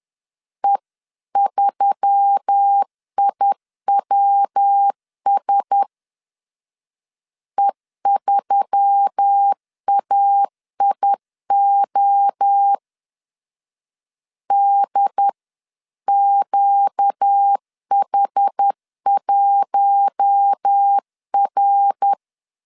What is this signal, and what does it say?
Morse code, heard as a keyed tone, about 10 wpm, 800 Hz. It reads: E3IWS E3AIO DQH1R
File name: LEZIONI / 6 PKX0 / 6 PK 10wpm 2436